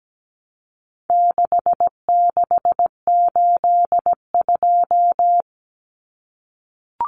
Morse code, heard as a keyed tone, about 17 wpm, 700 Hz. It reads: 6682
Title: Numbers - 4-Digit 17wpm